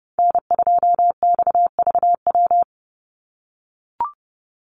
Morse code, heard as a keyed tone, about 30 wpm, 700 Hz. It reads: N2XVW
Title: USA Call Signs 30wpm